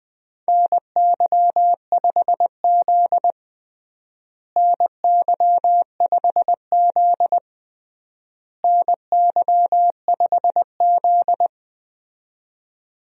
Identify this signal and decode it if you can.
Morse code, heard as a keyed tone, about 19 wpm, 700 Hz. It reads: NY5Z NY5Z NY5Z
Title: USA Call Signs - 4 Characters - Repeated 3x - 20wpm